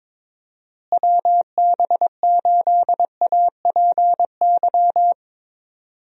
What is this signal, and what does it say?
Morse code, heard as a keyed tone, about 22 wpm, 700 Hz. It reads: WB8APY
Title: USA Call Signs - 6 Characters - Repeated 2x - 22wpm